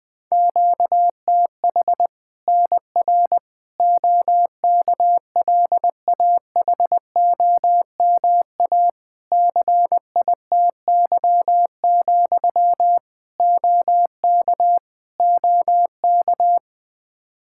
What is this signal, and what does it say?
Morse code, heard as a keyed tone, about 20 wpm, 700 Hz. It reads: QTH NR OKLAHOMA CITY, OK OK
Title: Practice QSO - Exchange 1 - Beginning - 20wpm